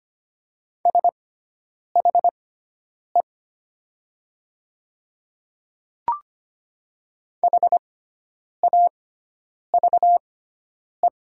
Morse code, heard as a keyed tone, about 25 wpm, spaced as for 9 wpm, 700 Hz. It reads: SHE HAVE
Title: 6x Character Spacing - Top 300 Words - 25wpm